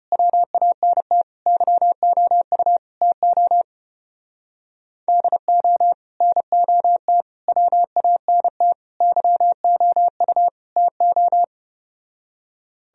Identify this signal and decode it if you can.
Morse code, heard as a keyed tone, about 33 wpm, 700 Hz. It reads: WANT YOU TO DO NOT WANT YOU TO